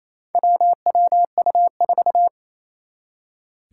Morse code, heard as a keyed tone, about 28 wpm, 700 Hz. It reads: WWU4
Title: Four Challenging Letter-Number Combinations 28wpm